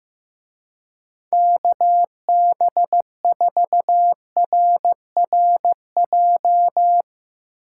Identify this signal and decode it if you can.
Morse code, heard as a keyed tone, about 15 wpm, 700 Hz. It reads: KB4RRJ
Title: Speed Racing - USA 2x3 Call Signs 15wpm